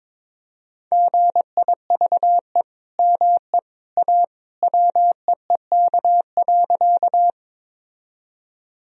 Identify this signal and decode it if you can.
Morse code, heard as a keyed tone, about 22 wpm, 700 Hz. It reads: GIVE ME A WEEK.
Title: Sentences from Top 1000 Words - Review Part 2 - 22wpm